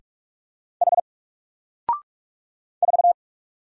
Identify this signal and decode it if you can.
Morse code, heard as a keyed tone, about 46 wpm, 700 Hz. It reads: H 4
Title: Single Letter-Number Alt 45wpm